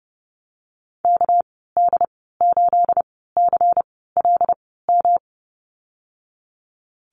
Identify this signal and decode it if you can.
Morse code, heard as a keyed tone, about 30 wpm, spaced as for 17 wpm, 700 Hz. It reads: KD8CLM
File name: Lesson 37g - Letter x - Call Signs 30wpm